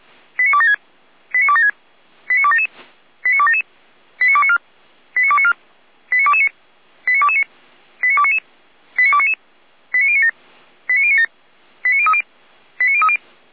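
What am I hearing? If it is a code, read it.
Five-tone selective call (ZVEI), heard as quick runs of five tones, tones of about 70 ms. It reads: 89276 89276 89277 89277 89294 89294 89229 89229 89299 89299 89097 89097 89030 89030
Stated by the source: ZVEI five-tone selective call received on an SDR.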